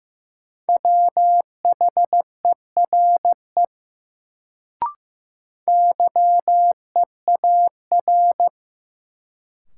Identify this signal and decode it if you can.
Morse code, heard as a keyed tone, about 15 wpm, 700 Hz. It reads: WHERE YEAR
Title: Sentences from Top 200 Words - New Word Drill - 15wpm